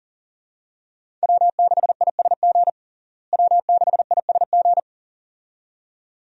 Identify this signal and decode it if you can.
Morse code, heard as a keyed tone, about 40 wpm, 700 Hz. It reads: W6ISG W6ISG
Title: USA Call Signs - 5 Characters - Repeated 2x - 40wpm